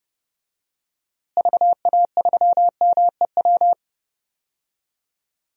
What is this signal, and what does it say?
Morse code, heard as a keyed tone, about 32 wpm, 700 Hz. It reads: VA3MEW
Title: DXCC-100 Call Signs - 6 Characters - Repeated 3x - 30wpm